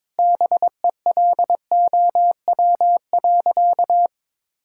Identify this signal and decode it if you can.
Morse code, heard as a keyed tone, about 21 wpm, 700 Hz. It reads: BELOW.
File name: Sentences with 9-Word Phrases - 22wpm